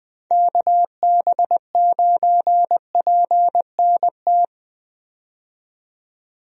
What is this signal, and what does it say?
Morse code, heard as a keyed tone, about 20 wpm, 700 Hz. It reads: KB9PNT
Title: USA Call Signs - 6 Characters - 20wpm